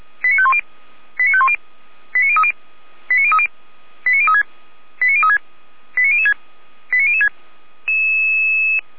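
Five-tone selective call (ZVEI), heard as quick runs of five tones, tones of about 70 ms. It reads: 89510 89510 89030 89030 89036 89036 89006 89006
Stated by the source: ZVEI five-tone selective call 89504, received by SDR.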